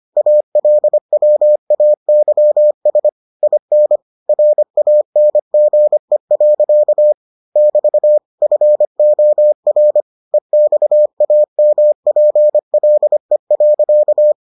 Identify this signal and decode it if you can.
Morse code, heard as a keyed tone, about 25 wpm, 600 Hz. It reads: ALWAYS IN RANGE. <BT> FOR EXAMPLE.